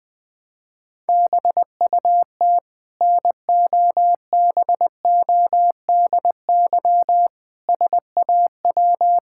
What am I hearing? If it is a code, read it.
Morse code, heard as a keyed tone, about 21 wpm, 700 Hz. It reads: BUT NOBODY SAW
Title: Sentences from Top 1600 Words 20wpm